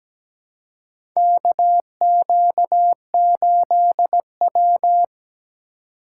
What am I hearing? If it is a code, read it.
Morse code, heard as a keyed tone, about 17 wpm, 700 Hz. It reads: KQ8W
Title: USA Call Signs - 4 Characters - 3x Character Spacing - 17wpm